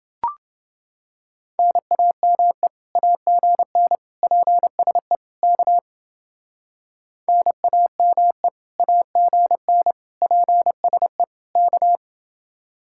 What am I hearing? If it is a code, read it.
Morse code, heard as a keyed tone, about 29 wpm, 700 Hz. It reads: NAME AGN PSE K NAME AGN PSE K